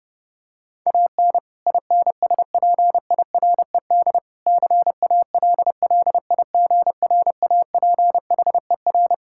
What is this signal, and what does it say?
Morse code, heard as a keyed tone, about 30 wpm, 700 Hz. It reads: AN INSPIRED CALLIGRAPHER
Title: Pangram Challenge v2 - 30wpm